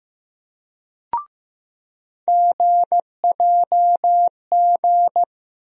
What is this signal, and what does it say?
Morse code, heard as a keyed tone, about 15 wpm, 700 Hz. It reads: GJG
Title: USA Call Sign Postfixes - 3 Letters 15wpm